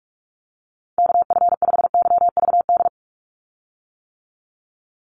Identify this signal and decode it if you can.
Morse code, heard as a keyed tone, about 46 wpm, 700 Hz. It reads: KF5YVD
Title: USA Call Signs 45wpm